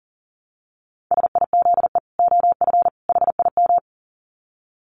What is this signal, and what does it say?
Morse code, heard as a keyed tone, about 40 wpm, 700 Hz. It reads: SIZE OF HIM
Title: Dit Challenge 40wpm